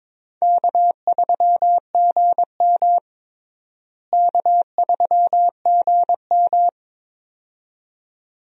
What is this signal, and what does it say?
Morse code, heard as a keyed tone, about 21 wpm, 700 Hz. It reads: K3GM K3GM
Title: USA Call Signs - 4 Characters - Repeated 2x - 22wpm